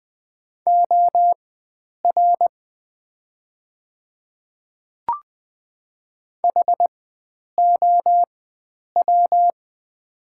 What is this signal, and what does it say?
Morse code, heard as a keyed tone, about 20 wpm, spaced as for 9 wpm, 690 Hz. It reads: OR HOW
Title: 4x Character Spacing - Top 300 Words - 20wpm